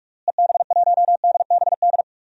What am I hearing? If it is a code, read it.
Morse code, heard as a keyed tone, about 46 wpm, 700 Hz. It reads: EB1DBD
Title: DXCC-100 Call Signs - 6 Characters - Repeated 3x - 45wpm